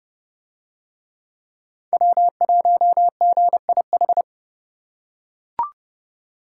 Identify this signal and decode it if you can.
Morse code, heard as a keyed tone, about 30 wpm, 700 Hz. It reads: W1GIH